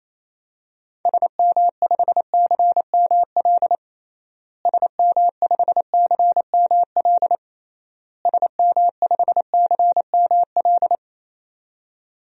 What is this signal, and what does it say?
Morse code, heard as a keyed tone, about 28 wpm, 700 Hz. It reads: SM5CML SM5CML SM5CML
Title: DXCC-100 Call Signs - 6 Characters - Repeated 3x - 28wpm